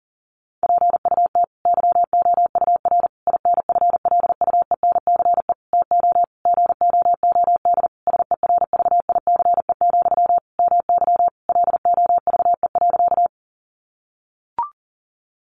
Morse code, heard as a keyed tone, about 40 wpm, 700 Hz. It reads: PUT YOUR INFLUENCE TO GOOD SERVICE, MY LOVE.
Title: Sentences from Top 1100 Words 40wpm